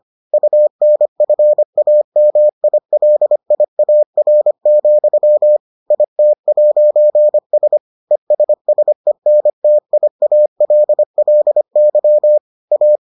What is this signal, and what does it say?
Morse code, heard as a keyed tone, about 25 wpm, 600 Hz. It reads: UNFAMILIAR, IT'S ESSENTIALLY A